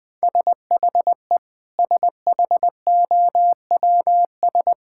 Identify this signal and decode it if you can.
Morse code, heard as a keyed tone, about 20 wpm, 700 Hz. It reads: SHE SHOWS
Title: Sentences with 4-Word Phrases - 20wpm